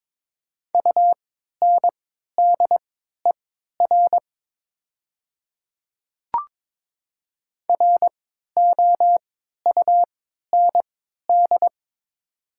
Morse code, heard as a keyed tone, about 22 wpm, spaced as for 13 wpm, 700 Hz. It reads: UNDER ROUND
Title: 3x Character Spacing - Top 200 Words - 22wpm